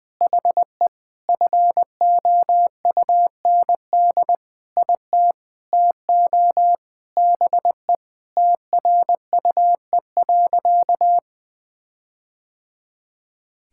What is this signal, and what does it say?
Morse code, heard as a keyed tone, about 19 wpm, 700 Hz. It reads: HE FOUND IT TO BE TRUE.
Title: Sentences with 4-Word Phrases - 20wpm